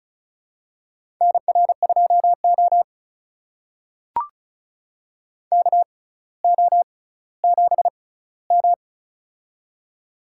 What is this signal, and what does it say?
Morse code, heard as a keyed tone, about 36 wpm, spaced as for 18 wpm, 700 Hz. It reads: NR2O KO7M
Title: USA Call Signs - 4 Characters - 6x Character Spacing - 35wpm